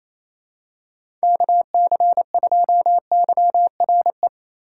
Morse code, heard as a keyed tone, about 28 wpm, 700 Hz. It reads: KC2YRE
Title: USA Call Signs - 6 Characters - 2x Character Spacing - 28wpm